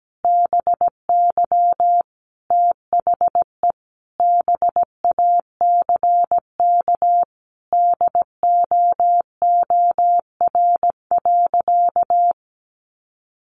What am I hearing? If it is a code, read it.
Morse code, heard as a keyed tone, about 17 wpm, 700 Hz. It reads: BY THE BACK DOOR.
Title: Sentences from Top 300 Words 17wpm